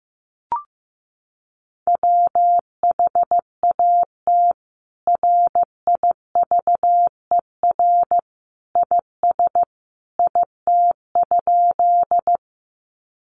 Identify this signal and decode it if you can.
Morse code, heard as a keyed tone, about 14 wpm, 700 Hz. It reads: WHAT RIVER IS IT?
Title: Sentences from Top 300 Words 15wpm